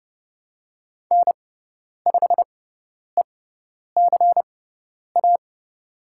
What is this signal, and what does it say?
Morse code, heard as a keyed tone, about 29 wpm, spaced as for 10 wpm, 700 Hz. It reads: N5ECA
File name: Lesson 16g - Letter c - Call Signs 30wpm